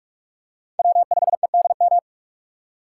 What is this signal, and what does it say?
Morse code, heard as a keyed tone, about 41 wpm, 700 Hz. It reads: W5EDM
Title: USA Call Signs - 5 Characters - 3x Character Spacing - 45wpm